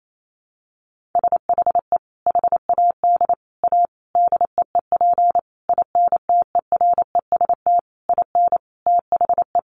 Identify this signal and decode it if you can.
Morse code, heard as a keyed tone, about 28 wpm, 700 Hz. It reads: SHE HAD A DEEP INTEREST IN THE
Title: Sentences from Top 1300 Words 28wpm